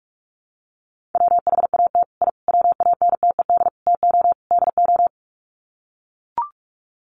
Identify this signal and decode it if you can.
Morse code, heard as a keyed tone, about 44 wpm, 700 Hz. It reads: WHAT I WANTED TO DO
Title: Sets of 5 Words Encore2 45wpm